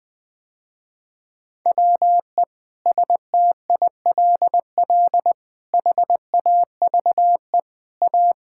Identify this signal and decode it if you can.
Morse code, heard as a keyed tone, about 21 wpm, 700 Hz. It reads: WE STILL HAVE A
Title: Sentences with 5-Word Phrases - Spoken Beginning - 20wpm